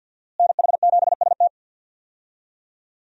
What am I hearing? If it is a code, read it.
Morse code, heard as a keyed tone, about 52 wpm, 700 Hz. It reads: NH7ST
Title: USA Call Signs - 5 Characters - Repeated 3x - 50wpm